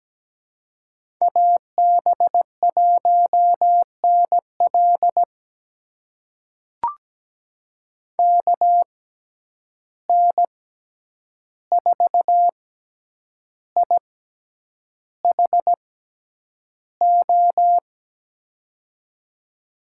Morse code, heard as a keyed tone, about 17 wpm, spaced as for 8 wpm, 700 Hz. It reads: AB1NL KN4IHO
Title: USA Call Signs - 6x Character Spacing - 17wpm